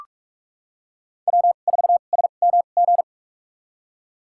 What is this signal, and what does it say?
Morse code, heard as a keyed tone, about 47 wpm, spaced as for 33 wpm, 700 Hz. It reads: W4SMG
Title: USA Call Signs - 5 Characters - 2x Character Spacing - 45wpm